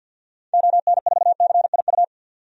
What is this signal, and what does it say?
Morse code, heard as a keyed tone, about 50 wpm, 700 Hz. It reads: ON4CIU
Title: DXCC-100 Call Signs - 6 Characters - 50wpm